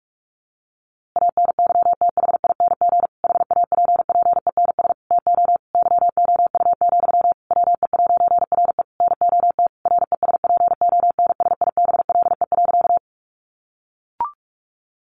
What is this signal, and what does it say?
Morse code, heard as a keyed tone, about 44 wpm, 700 Hz. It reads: ANYTHING HAPPENS TO YOU, WE'RE NOT RESPONSIBLE.
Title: A Sound Of Thunder 45wpm